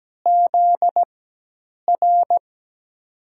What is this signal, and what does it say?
Morse code, heard as a keyed tone, about 17 wpm, spaced as for 8 wpm, 700 Hz. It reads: ZR
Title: USA Call Signs - 5 Characters - 4x Character Spacing - 17wpm